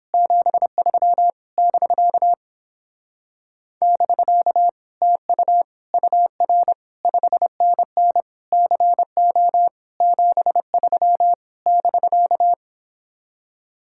Morse code, heard as a keyed tone, about 31 wpm, 700 Hz. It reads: 73 <BK> <BK> TU UR 5NN CO 73 <BK>